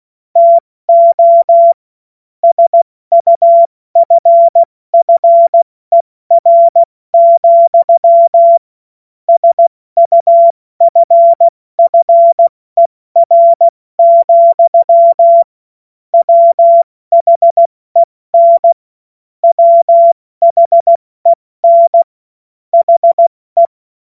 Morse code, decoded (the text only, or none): TO SUFFER, SUFFER, WHEN WHEN HE